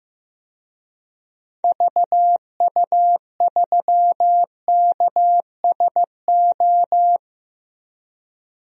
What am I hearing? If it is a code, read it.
Morse code, heard as a keyed tone, about 15 wpm, 700 Hz. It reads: VU3KSO
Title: DXCC-100 Call Signs - 6 Characters - Repeated 2x - 15wpm